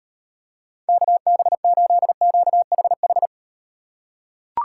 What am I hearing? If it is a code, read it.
Morse code, heard as a keyed tone, about 37 wpm, 700 Hz. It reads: KB8QHH